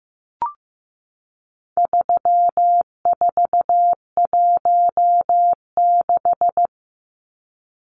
Morse code, heard as a keyed tone, about 15 wpm, 700 Hz. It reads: 3416
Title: Numbers - 4-Digit 15wpm